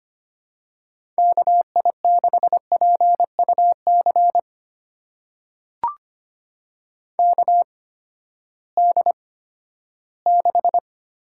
Morse code, heard as a keyed tone, about 25 wpm, spaced as for 14 wpm, 700 Hz. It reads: KI6PUC KD6
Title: USA Call Signs - 6 Characters - 8x Character Spacing - 25wpm